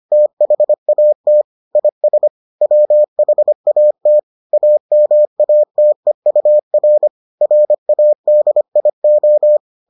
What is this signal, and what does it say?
Morse code, heard as a keyed tone, about 25 wpm, 600 Hz. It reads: THAT IS WHAT AMATEUR RADIO